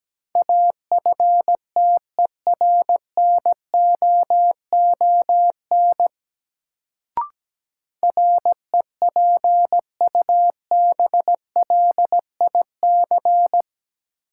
Morse code, heard as a keyed tone, about 17 wpm, 700 Hz. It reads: AFTERNOON REPUBLIC
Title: Sentences from Top 1500 Words - New Word Drill - 17wpm